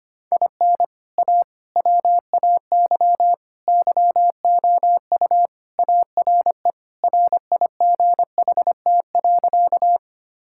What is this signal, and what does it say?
Morse code, heard as a keyed tone, about 25 wpm, 700 Hz. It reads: IN A WAY YOU ARE RIGHT.
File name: practice / 25wpm / Sentences with 3-Word Phrases - 25wpm